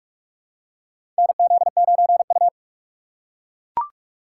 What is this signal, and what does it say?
Morse code, heard as a keyed tone, about 41 wpm, 700 Hz. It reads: NZ9U
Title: USA Call Signs - 4 Characters - 4x Character Spacing - 45wpm